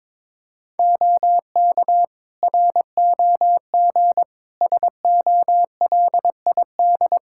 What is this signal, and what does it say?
Morse code, heard as a keyed tone, about 22 wpm, 700 Hz. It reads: OK ROG SOLID
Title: Practice QSO - Exchange 3 - More Info - 22wpm